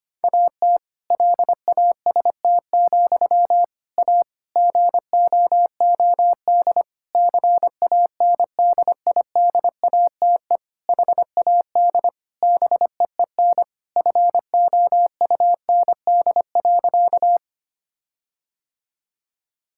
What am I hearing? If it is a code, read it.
Morse code, heard as a keyed tone, about 25 wpm, 700 Hz. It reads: AT LAST, A GOOD CANDIDATE HAD BEEN FOUND.
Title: Sentences from Top 1600 Words 25wpm